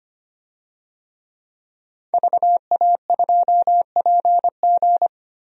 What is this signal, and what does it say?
Morse code, heard as a keyed tone, about 25 wpm, 700 Hz. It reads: VA2PG
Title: DXCC-100 Call Signs - 5 Characters - Repeated 3x - 25wpm